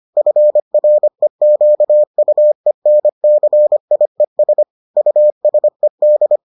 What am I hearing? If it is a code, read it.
Morse code, heard as a keyed tone, about 25 wpm, 600 Hz. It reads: FREQUENCIES USED